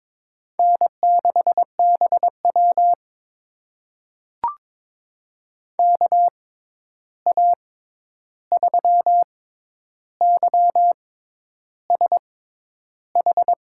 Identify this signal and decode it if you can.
Morse code, heard as a keyed tone, about 22 wpm, spaced as for 10 wpm, 700 Hz. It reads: N6BW KA3YSH
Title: USA Call Signs - 6x Character Spacing - 22wpm